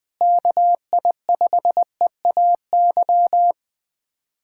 Morse code, heard as a keyed tone, about 20 wpm, 700 Hz. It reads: KI5EAY
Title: USA Callsign DE Callsign - 6 Characters - 20wpm